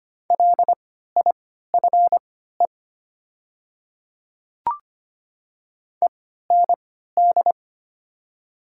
Morse code, heard as a keyed tone, about 25 wpm, spaced as for 14 wpm, 700 Hz. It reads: LIFE END
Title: 3x Character Spacing - Top 300 Words - 25wpm